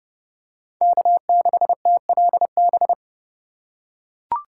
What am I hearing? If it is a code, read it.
Morse code, heard as a keyed tone, about 30 wpm, 700 Hz. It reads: K6TLB
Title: USA Call Signs - 2x Character Spacing - 30wpm